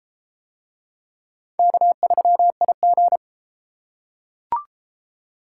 Morse code, heard as a keyed tone, about 30 wpm, 700 Hz. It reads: K3IG